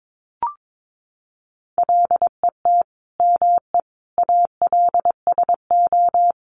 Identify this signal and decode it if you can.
Morse code, heard as a keyed tone, about 22 wpm, 700 Hz. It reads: LET ME ALSO
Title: Sentences from Top 400 Words 22wpm